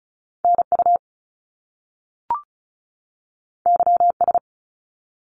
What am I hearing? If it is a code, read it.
Morse code, heard as a keyed tone, about 36 wpm, 700 Hz. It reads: NU YS
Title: Top Two Letter Combinations in Words 35wpm